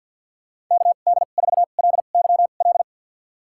Morse code, heard as a keyed tone, about 50 wpm, spaced as for 36 wpm, 690 Hz. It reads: KD4LYL
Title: USA Call Signs - 2x Character Spacing - 50wpm